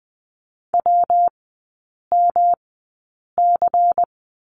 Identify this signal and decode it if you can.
Morse code, heard as a keyed tone, about 20 wpm, spaced as for 8 wpm, 700 Hz. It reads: WMC
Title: 114 Lesson 19e - Letter w - Three Characters 20wpm